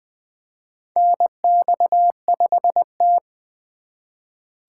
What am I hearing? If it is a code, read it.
Morse code, heard as a keyed tone, about 20 wpm, 700 Hz. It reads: NX5T
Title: USA Call Signs - 4 Characters - 4x Character Spacing - 20wpm